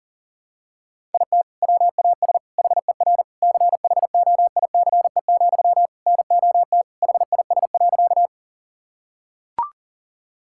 Morse code, heard as a keyed tone, about 40 wpm, 700 Hz. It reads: IT WAS HER CHOICE, NOT HIS.
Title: Sentences from Top 1000 Words - Review Part 6 - 40wpm